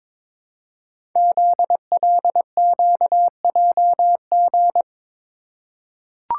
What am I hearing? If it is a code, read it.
Morse code, heard as a keyed tone, about 22 wpm, 700 Hz. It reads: ZLQJG
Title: Code Groups - 5 Letters - 22wpm